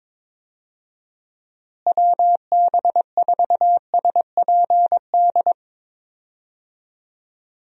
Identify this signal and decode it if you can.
Morse code, heard as a keyed tone, about 22 wpm, 700 Hz. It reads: WB4SPD
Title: USA Call Signs - 6 Characters - Repeated 2x - 22wpm